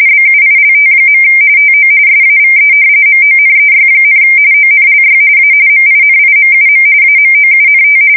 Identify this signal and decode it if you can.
RTTY, about 45.45 baud. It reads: BUILT. TO SAVE MONEY, ONLY A MINIMUM AMOUNT
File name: fsk170 / battleshipnj